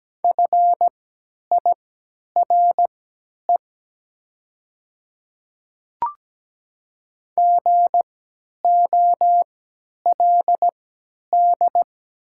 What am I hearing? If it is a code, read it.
Morse code, heard as a keyed tone, about 17 wpm, spaced as for 10 wpm, 700 Hz. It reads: FIRE GOLD
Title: 3x Character Spacing - Top 500 Words - 17wpm